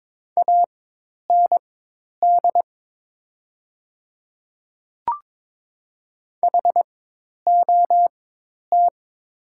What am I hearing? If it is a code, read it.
Morse code, heard as a keyed tone, about 22 wpm, spaced as for 10 wpm, 700 Hz. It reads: AND HOT
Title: 4x Character Spacing - Top 500 Words - 22wpm